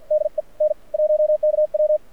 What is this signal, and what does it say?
Morse code, heard as a keyed tone, about 52 wpm, spaced as for 40 wpm, 590 Hz, with radio noise. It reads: DE N1KW
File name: archived / CRSnet